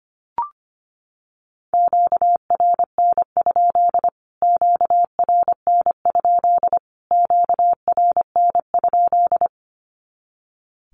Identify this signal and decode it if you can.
Morse code, heard as a keyed tone, about 25 wpm, 700 Hz. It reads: QRN? QRN? QRN?
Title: Q-Codes Repeated 3x 25wpm